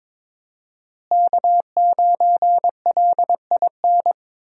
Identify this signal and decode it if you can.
Morse code, heard as a keyed tone, about 22 wpm, 700 Hz. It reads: K9LIN
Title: USA Call Signs - 5 Characters - Repeated 3x - 22wpm